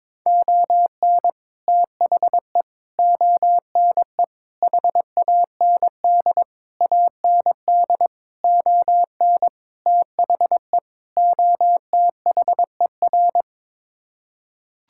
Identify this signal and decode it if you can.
Morse code, heard as a keyed tone, about 22 wpm, 700 Hz. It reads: ON THE ONE HAND AND ON THE OTHER
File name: practice / 22wpm / Sets of 8 Words - 22wpm